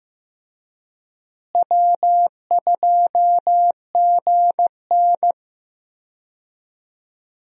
Morse code, heard as a keyed tone, about 15 wpm, 700 Hz. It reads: W2GN
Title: USA Call Signs - 4 Characters - Repeated 2x - 15wpm